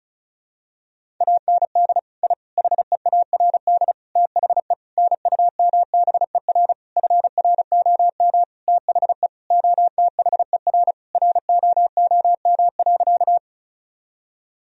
Morse code, heard as a keyed tone, about 34 wpm, 700 Hz. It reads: AND I HEARD THE NUMBER FROM THE OTHER ROOM.
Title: Sentences from Top 1000 Words - Review Part 1 - 35wpm